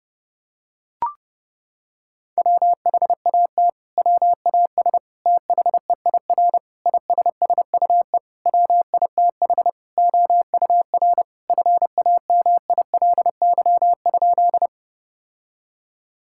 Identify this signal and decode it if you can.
Morse code, heard as a keyed tone, about 30 wpm, 700 Hz. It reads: WHAT WAS THEIR ISSUE WITH OUR FAMILY?
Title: Sentences from Top 1000 Words - Review Part 6 - 30wpm